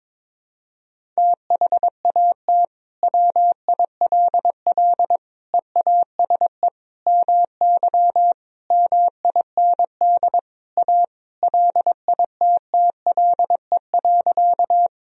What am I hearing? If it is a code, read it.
Morse code, heard as a keyed tone, about 22 wpm, 700 Hz. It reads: THAT WILL EASE MY MIND A LITTLE.
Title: Sentences from Top 1000 Words - Review Part 1 - 22wpm